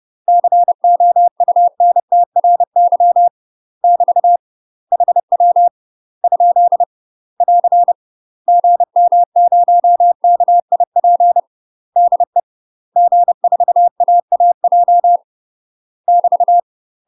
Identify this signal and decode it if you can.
Morse code, heard as a keyed tone, about 30 wpm, 700 Hz. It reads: COUNTRY <BT> HW ? <AR> GM0KIP DE G4AAJ <BT>
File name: UK MASTER QSOs TRAINING / 30WPM / UK MASTER QSOs 0009